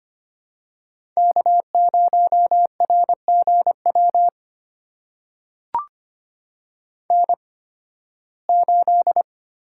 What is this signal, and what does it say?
Morse code, heard as a keyed tone, about 25 wpm, spaced as for 16 wpm, 700 Hz. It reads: K0RGW N8
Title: USA Call Signs - 5 Characters - 8x Character Spacing - 25wpm